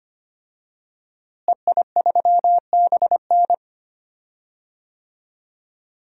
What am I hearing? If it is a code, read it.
Morse code, heard as a keyed tone, about 25 wpm, 700 Hz. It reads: EI3BN
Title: DXCC-100 Call Signs - 5 Characters - Repeated 3x - 25wpm